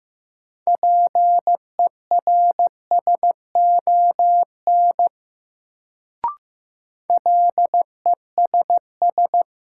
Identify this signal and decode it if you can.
Morse code, heard as a keyed tone, about 15 wpm, 700 Hz. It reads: PERSON LESS
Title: Sentences from Top 400 Words - New Word Drill - 15wpm